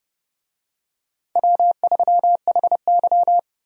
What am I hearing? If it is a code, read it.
Morse code, heard as a keyed tone, about 31 wpm, 700 Hz. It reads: W3HY
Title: USA Call Signs - 4 Characters - 30wpm